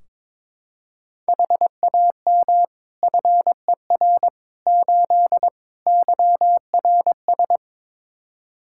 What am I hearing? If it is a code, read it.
Morse code, heard as a keyed tone, about 22 wpm, 700 Hz. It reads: HAM FER 8 YRS
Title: Practice QSO - Exchange 3 - More Info - 22wpm